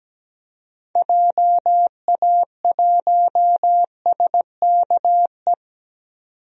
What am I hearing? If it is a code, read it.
Morse code, heard as a keyed tone, about 17 wpm, 700 Hz. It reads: JA1SKE
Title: DXCC-100 Call Signs - 6 Characters - 17wpm